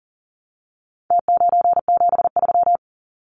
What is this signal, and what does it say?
Morse code, heard as a keyed tone, about 40 wpm, 700 Hz. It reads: T973
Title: Numbers - Contest Style - 4-Digit 40wpm